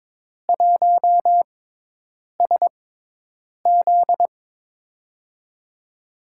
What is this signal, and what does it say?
Morse code, heard as a keyed tone, about 22 wpm, spaced as for 8 wpm, 700 Hz. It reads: 1SZ
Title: USA Call Signs - 6x Character Spacing - 22wpm